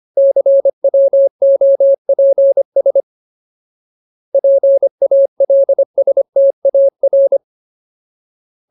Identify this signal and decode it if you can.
Morse code, heard as a keyed tone, about 25 wpm, 550 Hz. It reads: CWOPS PALSTAR
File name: Ultimate Word List / 25WPM Ultimate